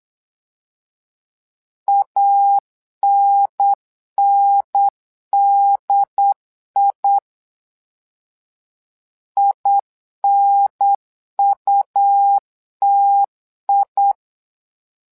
Morse code, heard as a keyed tone, about 8 wpm, 800 Hz. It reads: ANNDI INUTI
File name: LEZIONI / 2 UHDN / 2 UHDN 8wpm2232